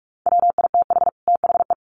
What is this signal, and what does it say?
Morse code, heard as a keyed tone, about 44 wpm, 700 Hz. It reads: WITH THE
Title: Sets of 2 Words 45wpm